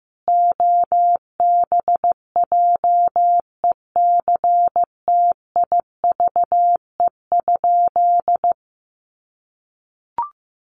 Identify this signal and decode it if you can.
Morse code, heard as a keyed tone, about 15 wpm, 700 Hz. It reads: OBJECTIVE?
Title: Sentences from Top 1200 Words 15wpm